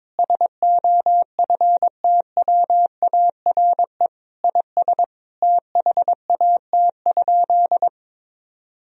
Morse code, heard as a keyed tone, about 21 wpm, 700 Hz. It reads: SOFTWARE IS THAT?
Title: Sentences from Top 1000 Words - Review Part 3 - 22wpm